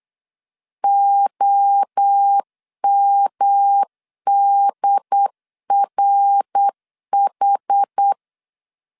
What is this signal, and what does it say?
Morse code, heard as a keyed tone, about 8 wpm, 800 Hz. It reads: OMDRH
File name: CW / DET / 1 3 1500C S B 8wpm 4141